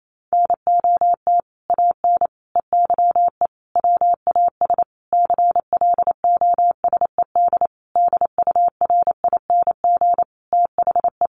Morse code, heard as a keyed tone, about 28 wpm, 700 Hz. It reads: NOT AN EYE WAS CLOSED DURING THE
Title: Sentences from Top 1400 Words 28wpm